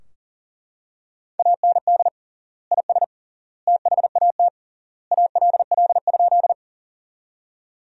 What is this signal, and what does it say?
Morse code, heard as a keyed tone, about 40 wpm, 700 Hz. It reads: AND IS THAT ALL?
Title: 3x Word-Spacing - Sentences from Top 100 Words - 40wpm